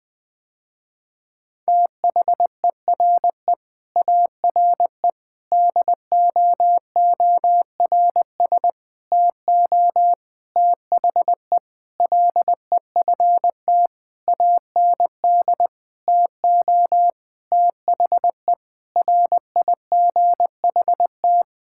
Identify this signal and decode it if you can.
Morse code, heard as a keyed tone, about 20 wpm, 700 Hz. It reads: THERE ARE DOORS TO THE LEFT AND TO THE RIGHT
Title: Sentences with 8-Word Phrases - Spoken Beginning - 20wpm